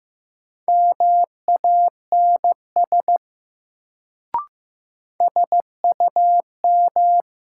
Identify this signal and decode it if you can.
Morse code, heard as a keyed tone, about 15 wpm, 700 Hz. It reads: MANS SUM
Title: Sentences from Top 1000 Words - Review Part 6 - New Word Drill - 15wpm